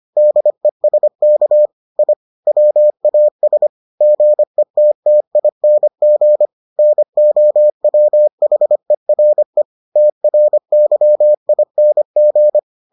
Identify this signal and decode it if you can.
Morse code, heard as a keyed tone, about 25 wpm, 600 Hz. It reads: DESK I WAS GETTING NOWHERE TRYING